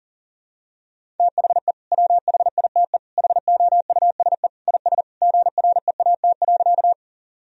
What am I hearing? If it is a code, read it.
Morse code, heard as a keyed tone, about 40 wpm, 700 Hz. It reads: THE WHITE HOUSE IS GREAT.
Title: Sentences from Top 1000 Words - Review Part 1 - 40wpm